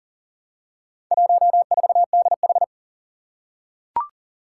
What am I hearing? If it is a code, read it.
Morse code, heard as a keyed tone, about 38 wpm, 700 Hz. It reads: 14DH
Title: Four Challenging Letter-Number Combinations 40wpm